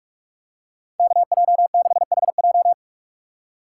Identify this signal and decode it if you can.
Morse code, heard as a keyed tone, about 42 wpm, 700 Hz. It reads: KJ6HJ